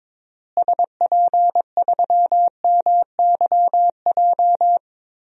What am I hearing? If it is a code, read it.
Morse code, heard as a keyed tone, about 23 wpm, 700 Hz. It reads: SP3MYJ
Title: DXCC-100 Call Signs - 6 Characters - Repeated 3x - 22wpm